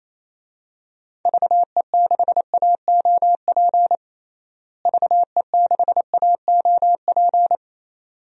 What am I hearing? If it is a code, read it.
Morse code, heard as a keyed tone, about 28 wpm, 700 Hz. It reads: VE6AOP VE6AOP
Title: DXCC-100 Call Signs - 6 Characters - Repeated 2x - 28wpm